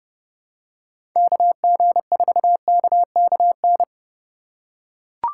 Morse code, heard as a keyed tone, about 29 wpm, 700 Hz. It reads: KG4KKN
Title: USA Call Signs - 6 Characters - 3x Character Spacing - 30wpm